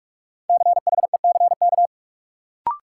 Morse code, heard as a keyed tone, about 48 wpm, 700 Hz. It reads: CHECK
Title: Sentences from Top 1400 Words - New Word Drill Alt - 45wpm